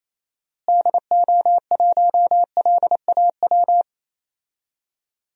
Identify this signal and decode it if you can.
Morse code, heard as a keyed tone, about 28 wpm, 700 Hz. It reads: DO1LAW
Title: DXCC-100 Call Signs - 6 Characters - Repeated 2x - 28wpm